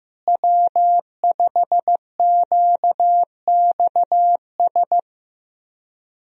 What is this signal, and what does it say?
Morse code, heard as a keyed tone, about 15 wpm, 700 Hz. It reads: W5QXS
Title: USA Call Signs - 5 Characters - Repeated 3x - 15wpm